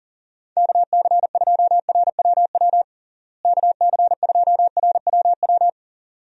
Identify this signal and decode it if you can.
Morse code, heard as a keyed tone, about 40 wpm, 700 Hz. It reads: KC2RWW KC2RWW
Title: USA Call Signs - 6 Characters - Repeated 2x - 40wpm